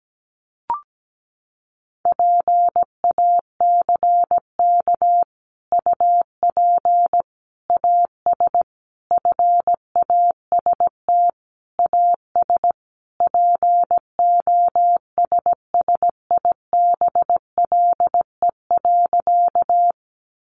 Morse code, heard as a keyed tone, about 17 wpm, 700 Hz. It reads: PACK UP AS FAST AS POSSIBLE.
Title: Sentences from Top 1300 Words 17wpm